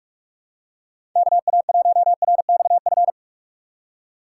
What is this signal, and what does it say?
Morse code, heard as a keyed tone, about 46 wpm, 700 Hz. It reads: KA1RXF
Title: USA Call Signs - 4x Character Spacing - 45wpm